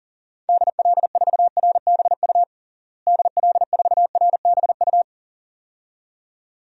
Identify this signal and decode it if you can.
Morse code, heard as a keyed tone, about 40 wpm, 700 Hz. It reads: DL4RBU DL4RBU
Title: DXCC-100 Call Signs - 6 Characters - Repeated 2x - 40wpm